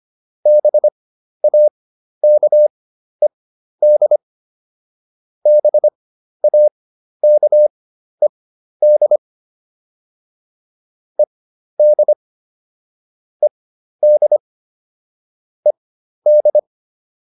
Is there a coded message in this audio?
Yes: Morse code from 0.5 to 16.6 s, heard as a keyed tone, about 25 wpm, spaced as for 12 wpm, 600 Hz. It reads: BAKED BAKED ED ED ED